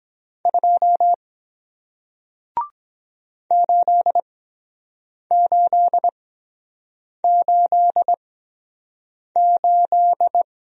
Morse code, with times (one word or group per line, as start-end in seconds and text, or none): 0.5-1.1 2
3.5-10.4 8888